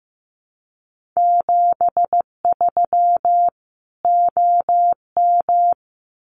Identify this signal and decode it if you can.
Morse code, heard as a keyed tone, about 15 wpm, 700 Hz. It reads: 73 OM
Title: Common QSO Phrases 15wpm